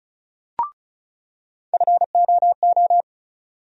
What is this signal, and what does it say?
Morse code, heard as a keyed tone, about 36 wpm, 700 Hz. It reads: FOO
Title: USA Call Sign Postfixes - 3 Letters 35wpm